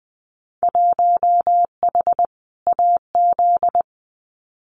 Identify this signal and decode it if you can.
Morse code, heard as a keyed tone, about 19 wpm, 700 Hz. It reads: 1H AZ
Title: Winter Field Day Exchanges 20wpm